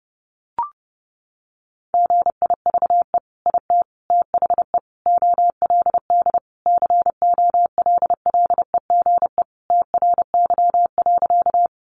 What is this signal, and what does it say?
Morse code, heard as a keyed tone, about 30 wpm, 700 Hz. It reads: GIVE IT THE OLD COLLEGE TRY.
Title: Sentences from Top 1300 Words 30wpm